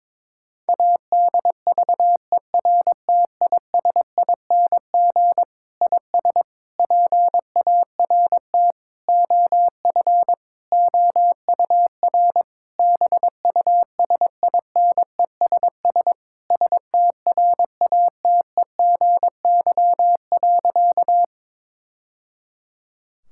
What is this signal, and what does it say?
Morse code, heard as a keyed tone, about 21 wpm, 700 Hz. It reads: ADVERTISING IS PART OF OUR BUSINESS STRATEGY.
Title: Sentences from Top 1000 Words - Review Part 6 - 22wpm